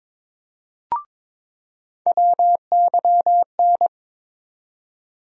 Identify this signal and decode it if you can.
Morse code, heard as a keyed tone, about 22 wpm, 700 Hz. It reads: WYN
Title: USA Call Sign Postfixes - 3 Letters 22wpm